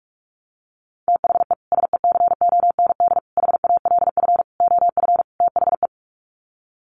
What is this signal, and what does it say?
Morse code, heard as a keyed tone, about 44 wpm, 700 Hz. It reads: THE SECOND HALF OF THE
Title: Sets of 5 Words 45wpm